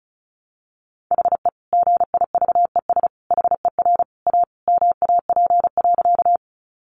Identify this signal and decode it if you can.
Morse code, heard as a keyed tone, about 34 wpm, 700 Hz. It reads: HE GIVES HER A MAP.
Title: Additional Easy Phrases 35wpm